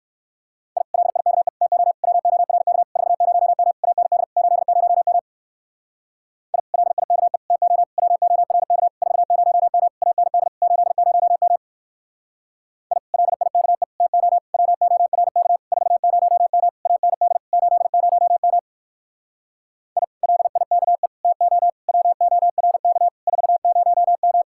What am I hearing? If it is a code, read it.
Morse code, heard as a keyed tone, about 56 wpm, 700 Hz. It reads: I LIKE TO WORK 40M AND 80M I LIKE TO WORK 40M AND 80M I LIKE TO WORK 40M AND 80M I LIKE TO WORK 40M